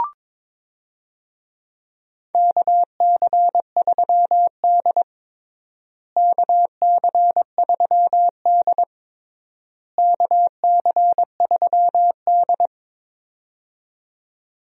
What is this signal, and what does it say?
Morse code, heard as a keyed tone, about 22 wpm, 700 Hz. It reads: KC3D KC3D KC3D
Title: USA Call Signs - 4 Characters - Repeated 3x - 22wpm